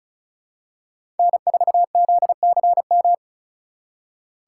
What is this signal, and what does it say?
Morse code, heard as a keyed tone, about 34 wpm, 700 Hz. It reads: N4ZCM
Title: USA Call Signs - 2x Character Spacing - 35wpm